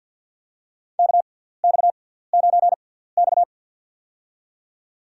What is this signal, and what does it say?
Morse code, heard as a keyed tone, about 53 wpm, spaced as for 17 wpm, 700 Hz. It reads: KX9X
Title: USA Call Signs - 6x Character Spacing - 50wpm